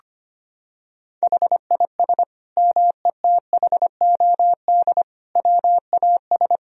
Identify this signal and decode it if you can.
Morse code, heard as a keyed tone, about 25 wpm, 700 Hz. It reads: HIS METHOD WAS
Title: Sentences from Top 1000 Words - Review Part 3 - 25wpm